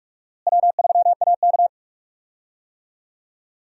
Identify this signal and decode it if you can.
Morse code, heard as a keyed tone, about 42 wpm, 700 Hz. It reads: W3AK
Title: USA Call Signs - 4 Characters - Repeated 2x - 45wpm